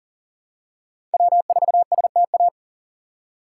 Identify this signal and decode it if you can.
Morse code, heard as a keyed tone, about 40 wpm, 700 Hz. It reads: W4STA
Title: USA Call Signs - 5 Characters - 4x Character Spacing - 40wpm